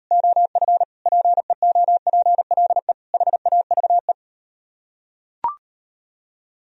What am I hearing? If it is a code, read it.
Morse code, heard as a keyed tone, about 38 wpm, 700 Hz. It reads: OF PEOPLE HAVE